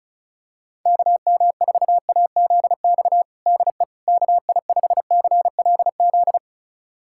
Morse code, heard as a keyed tone, about 36 wpm, 700 Hz. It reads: KM4AZX DE KI5CLZ
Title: USA Callsign DE Callsign - 6 Characters - 35wpm